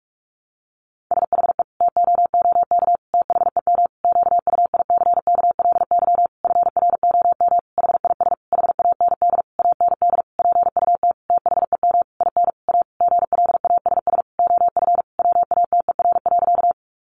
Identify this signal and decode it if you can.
Morse code, heard as a keyed tone, about 44 wpm, 700 Hz. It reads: SHE TOOK THEM QUICKLY FROM HIS HAND AND PUT THEM IN A GLASS OF WATER.
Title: Sentences from Top 1300 Words 45wpm